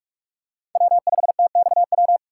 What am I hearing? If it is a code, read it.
Morse code, heard as a keyed tone, about 46 wpm, 700 Hz. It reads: W5TXW
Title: USA Call Signs - 5 Characters - 6x Character Spacing - 45wpm